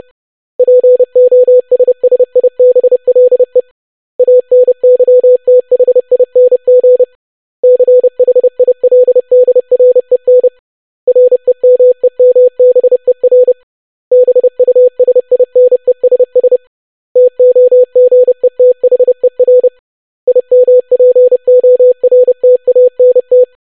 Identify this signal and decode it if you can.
Morse code, heard as a keyed tone, about 30 wpm, 500 Hz. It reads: POSSIBLE ANYTHING CHILDREN REMEMBER BUSINESS TOGETHER IMPORTANT
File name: basic500-25wpm